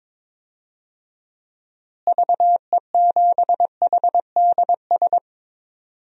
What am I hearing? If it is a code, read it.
Morse code, heard as a keyed tone, about 22 wpm, 700 Hz. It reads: VE7HDS
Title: DXCC-100 Call Signs - 6 Characters - Repeated 2x - 22wpm